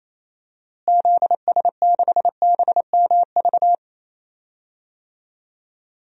Morse code, heard as a keyed tone, about 28 wpm, 700 Hz. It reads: ZS6BMV
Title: DXCC-100 Call Signs - 6 Characters - Repeated 3x - 28wpm